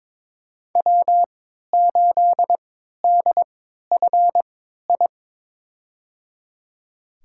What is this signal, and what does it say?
Morse code, heard as a keyed tone, about 22 wpm, spaced as for 12 wpm, 700 Hz. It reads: W8DFI
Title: USA Call Signs - 3x Character Spacing - 22wpm